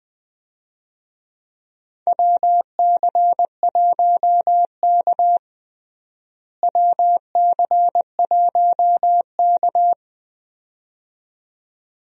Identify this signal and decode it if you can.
Morse code, heard as a keyed tone, about 20 wpm, 700 Hz. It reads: WC1K WC1K
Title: USA Call Signs - 4 Characters - Repeated 2x - 20wpm